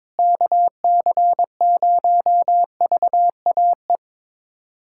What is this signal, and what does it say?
Morse code, heard as a keyed tone, about 22 wpm, 700 Hz. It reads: KC0VAE
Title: USA Callsign DE Callsign - 6 Characters - 22wpm